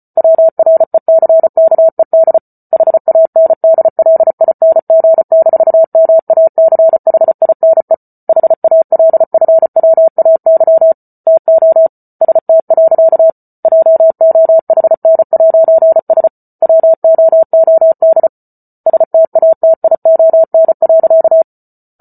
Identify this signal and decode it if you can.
Morse code, heard as a keyed tone, about 34 wpm, 650 Hz. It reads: WRECKED HANDLING-MACHINE HALFWAY TO ST. JOHN'S WOOD STATION.